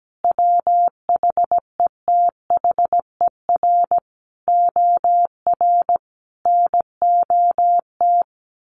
Morse code, heard as a keyed tone, about 17 wpm, 700 Hz. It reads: WHETHER OR NOT
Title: Sets of 5 Words Encore 17wpm